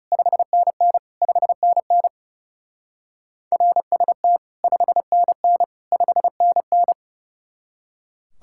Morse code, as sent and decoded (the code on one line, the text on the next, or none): ..... -. -. / ..... -. -. / .-. ... - / ..... -. -. / ..... -. -.
5NN 5NN RST 5NN 5NN